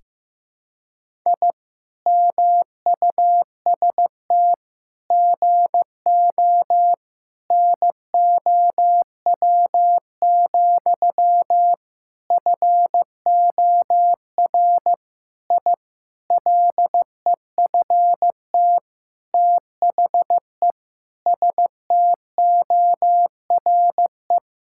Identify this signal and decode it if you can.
Morse code, heard as a keyed tone, about 15 wpm, 700 Hz. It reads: I MUST GO NOW, FOR I LEFT THE STORE
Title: Sentences from Top 1000 Words - Review Part 5 - 15wpm